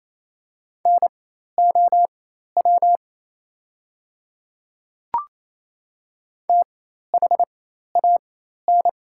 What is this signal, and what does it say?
Morse code, heard as a keyed tone, about 28 wpm, spaced as for 13 wpm, 690 Hz. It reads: NOW THAN
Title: 4x Character Spacing - Top 200 Words - 28wpm